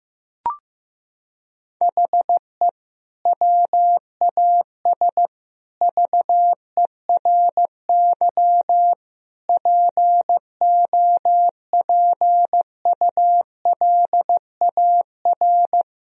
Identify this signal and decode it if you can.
Morse code, heard as a keyed tone, about 15 wpm, 700 Hz. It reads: HE WAS VERY POPULAR
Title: Sentences from Top 1000 Words - Review Part 3 - 15wpm